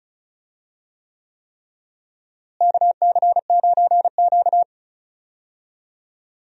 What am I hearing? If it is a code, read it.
Morse code, heard as a keyed tone, about 34 wpm, 700 Hz. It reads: KC9Q
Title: USA Call Signs - 4 Characters - Repeated 3x - 35wpm